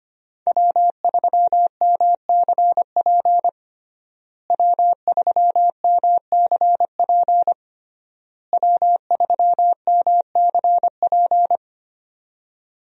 Morse code, heard as a keyed tone, about 25 wpm, 700 Hz. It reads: W3MCP W3MCP W3MCP
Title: USA Call Signs - 5 Characters - Repeated 3x - 25wpm